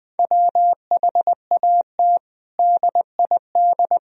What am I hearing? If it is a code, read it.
Morse code, heard as a keyed tone, about 20 wpm, 700 Hz. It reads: WHAT DID
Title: Sentences from Top 1000 Words - Review Part 3 - 20wpm